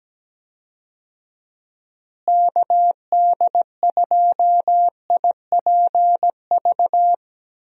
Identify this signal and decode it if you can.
Morse code, heard as a keyed tone, about 17 wpm, 700 Hz. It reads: KD2IPV
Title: USA Call Signs - 6 Characters - Repeated 3x - 17wpm